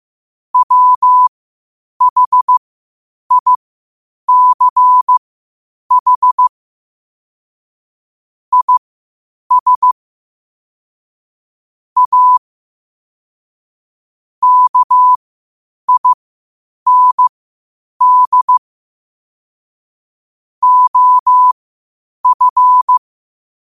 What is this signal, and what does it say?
Morse code, heard as a keyed tone, about 15 wpm, spaced as for 8 wpm, 1000 Hz. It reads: WHICH IS A KIND OF
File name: qotdfiles / 8wpm